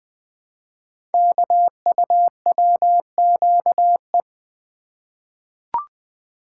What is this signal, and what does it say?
Morse code, heard as a keyed tone, about 20 wpm, 700 Hz. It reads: KUWQE
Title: Code Groups - 5 Letters - 20wpm